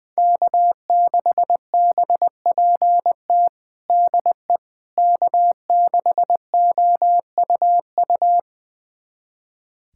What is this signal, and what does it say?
Morse code, heard as a keyed tone, about 20 wpm, 700 Hz. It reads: K6BPT DE K6OUU
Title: USA Callsign DE Callsign - 5 Characters - 20wpm